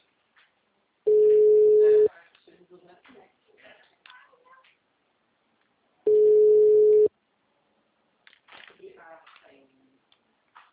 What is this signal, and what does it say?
Ringback tone (Europe, 425 Hz), with speech.
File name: out- -2006- - - .94109